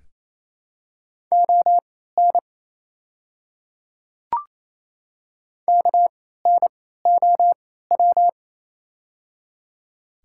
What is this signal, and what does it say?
Morse code, heard as a keyed tone, about 28 wpm, spaced as for 16 wpm, 700 Hz. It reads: ON KNOW
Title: 3x Character Spacing - Top 100 Words - 28wpm